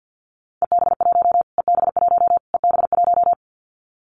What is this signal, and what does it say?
Morse code, heard as a keyed tone, about 50 wpm, 700 Hz. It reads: EB1 EB1 EB1
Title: Callsign Prefixes for Top 25 DXCC - Repeated 3x - 50wpm